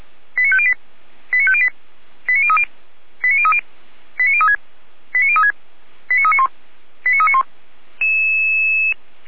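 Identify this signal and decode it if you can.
Five-tone selective call (ZVEI), heard as quick runs of five tones, tones of about 70 ms. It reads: 89508 89508 89030 89030 89036 89036 89391 89391
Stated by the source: ZVEI five-tone selective call received by SDR.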